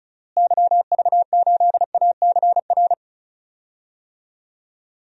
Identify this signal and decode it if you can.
Morse code, heard as a keyed tone, about 36 wpm, 700 Hz. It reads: YV8ACR
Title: DXCC-100 Call Signs - 6 Characters - Repeated 3x - 35wpm